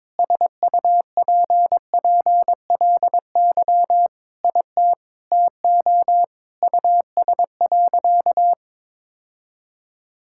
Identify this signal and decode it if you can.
Morse code, heard as a keyed tone, about 22 wpm, 700 Hz. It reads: SUPPLY IT TO US.
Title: Sentences from Top 1000 Words - Review Part 5 - 22wpm